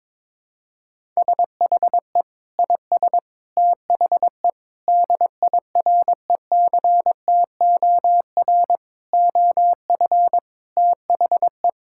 Morse code, heard as a keyed tone, about 22 wpm, 700 Hz. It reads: SHE IS THE DIRECTOR OF THE
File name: practice / 22wpm / Sentences with 3-Word Phrases - 22wpm